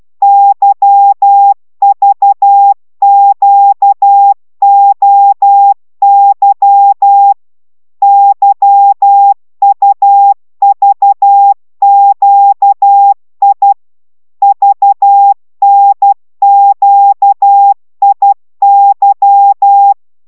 Morse code, heard as a keyed tone, about 12 wpm, 800 Hz. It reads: YVQOY YUVQI VNQIY